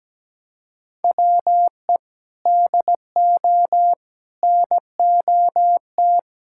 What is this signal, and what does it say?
Morse code, heard as a keyed tone, about 17 wpm, 700 Hz. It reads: WE DO NOT
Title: Sentences from Top 1600 Words 17wpm